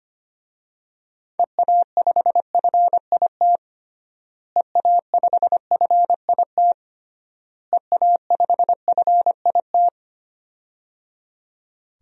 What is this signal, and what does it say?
Morse code, heard as a keyed tone, about 25 wpm, 700 Hz. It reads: EA5FIT EA5FIT EA5FIT
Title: DXCC-100 Call Signs - 6 Characters - Repeated 3x - 25wpm